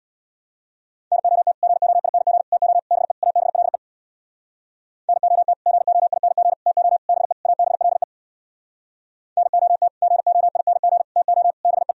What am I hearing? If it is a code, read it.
Morse code, heard as a keyed tone, about 66 wpm, 700 Hz. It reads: NOT GOING TO BE ABLE NOT GOING TO BE ABLE NOT GOING TO BE